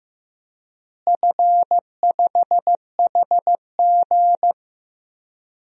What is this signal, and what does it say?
Morse code, heard as a keyed tone, about 15 wpm, 700 Hz. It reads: F5HG
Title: DXCC-100 Call Signs - 4 Characters - Repeated 3x - 15wpm